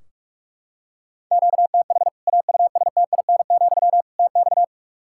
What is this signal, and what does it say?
Morse code, heard as a keyed tone, about 41 wpm, 700 Hz. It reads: QTH AUSTIN, TX
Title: Practice QSO - Exchange 1 - Beginning - 45wpm